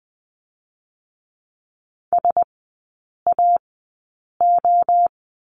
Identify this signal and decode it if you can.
Morse code, heard as a keyed tone, about 20 wpm, spaced as for 8 wpm, 700 Hz. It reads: SAO
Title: 053 Lesson 10e - Letter r - Three Characters 20wpm